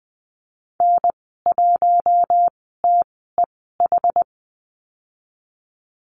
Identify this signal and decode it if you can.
Morse code, heard as a keyed tone, about 20 wpm, spaced as for 14 wpm, 700 Hz. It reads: N1TEH
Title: 076 Lesson 13g - Letter L - Call Signs 20wpm